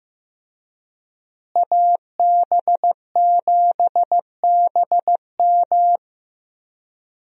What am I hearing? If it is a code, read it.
Morse code, heard as a keyed tone, about 15 wpm, 700 Hz. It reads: AB7BM
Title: USA Call Signs - 5 Characters - Repeated 3x - 15wpm